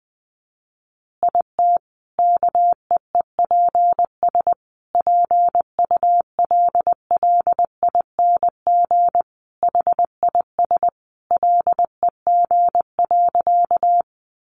Morse code, heard as a keyed tone, about 20 wpm, 700 Hz. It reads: IT KEEPS PULLING HIS LEG.
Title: Additional Easy Phrases 20wpm